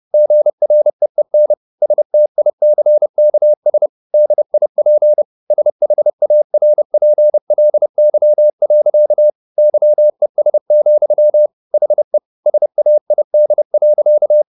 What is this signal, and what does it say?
Morse code, heard as a keyed tone, about 30 wpm, 600 Hz. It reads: GREEN STICKS DIP SHARPLY. YES, HE SAID.
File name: the old man and the sea / the old man and the sea 1 30wpm